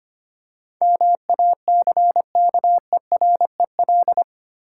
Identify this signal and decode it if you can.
Morse code, heard as a keyed tone, about 25 wpm, 700 Hz. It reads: MACKEREL
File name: practice / 25wpm / Difficult Words with Hints - 25wpm